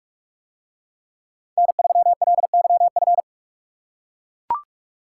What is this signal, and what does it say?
Morse code, heard as a keyed tone, about 44 wpm, 700 Hz. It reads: N3LYF